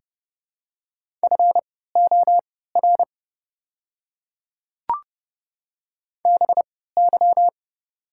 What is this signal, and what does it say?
Morse code, heard as a keyed tone, about 30 wpm, spaced as for 17 wpm, 700 Hz. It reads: FOR BY
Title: 3x Character Spacing - Top 100 Words - 30wpm